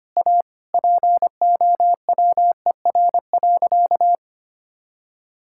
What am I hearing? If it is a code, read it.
Morse code, heard as a keyed tone, about 25 wpm, 700 Hz. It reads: A POWER.
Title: Sentences from Top 1000 Words - Review Part 5 - 25wpm